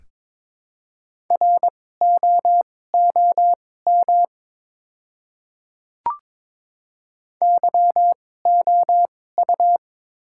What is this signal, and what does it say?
Morse code, heard as a keyed tone, about 22 wpm, spaced as for 16 wpm, 700 Hz. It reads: ROOM YOU
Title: 2x Character Spacing - Top 300 Words - 22wpm